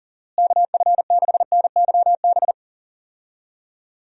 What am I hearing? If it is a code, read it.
Morse code, heard as a keyed tone, about 40 wpm, 700 Hz. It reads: KF6NYB
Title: USA Call Signs - 6 Characters - Repeated 3x - 40wpm